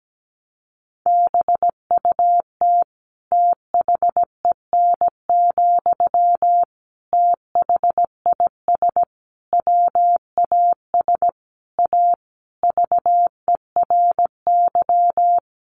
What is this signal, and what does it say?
Morse code, heard as a keyed tone, about 17 wpm, 700 Hz. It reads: BUT THEN, THIS WAS A VERY
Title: Sentences from Top 1400 Words 17wpm